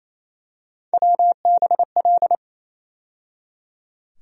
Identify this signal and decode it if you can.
Morse code, heard as a keyed tone, about 28 wpm, 700 Hz. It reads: WBL
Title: USA Call Sign Postfixes - 3 Letters 28wpm